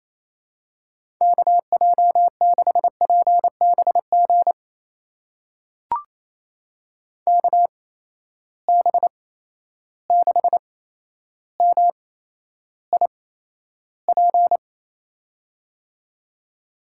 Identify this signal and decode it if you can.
Morse code, heard as a keyed tone, about 28 wpm, spaced as for 12 wpm, 700 Hz. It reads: KJ6PBG KB6MIP
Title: USA Call Signs - 6 Characters - 8x Character Spacing - 28wpm